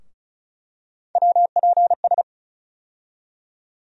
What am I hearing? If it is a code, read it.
Morse code, heard as a keyed tone, about 34 wpm, 700 Hz. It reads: WPS
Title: USA Call Sign Postfixes - 3 Letters 35wpm